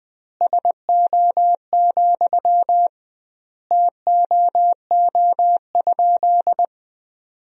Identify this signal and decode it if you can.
Morse code, heard as a keyed tone, about 20 wpm, 700 Hz. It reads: SO, TOO?
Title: 2x Word-Spacing - Sentences from Top 200 Words - 20wpm